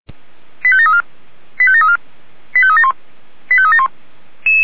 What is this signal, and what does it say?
Five-tone selective call (ZVEI), heard as quick runs of five tones, tones of about 70 ms. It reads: 85734 85734 85371 85371